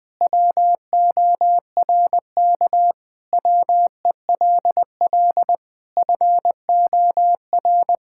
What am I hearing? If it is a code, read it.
Morse code, heard as a keyed tone, about 20 wpm, 700 Hz. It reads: WORK WELL FOR
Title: Sentences from Top 1000 Words - Review Part 6 - 20wpm